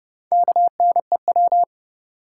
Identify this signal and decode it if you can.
Morse code, heard as a keyed tone, about 32 wpm, 700 Hz. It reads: KNEW
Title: Sentences from Top 400 Words - New Word Drill - 30wpm